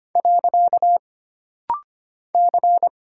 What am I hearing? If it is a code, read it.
Morse code, heard as a keyed tone, about 25 wpm, 700 Hz. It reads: . C
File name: Single Character (Letter:Number:Punctuation:Prosign) - Rapid-Fire - 25wpm